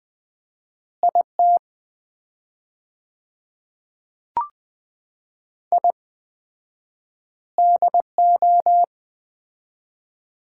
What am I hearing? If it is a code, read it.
Morse code, heard as a keyed tone, about 20 wpm, 700 Hz. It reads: IT I DO
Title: 4x Word-Spacing - Sets of 4 Words - 20wpm